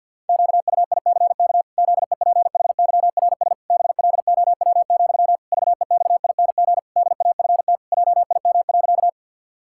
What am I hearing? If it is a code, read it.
Morse code, heard as a keyed tone, about 50 wpm, 700 Hz. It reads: QUICK ZEPHYRS BLOW, VEXING DAFT JIM.
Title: Pangram Challenge v2 - 50wpm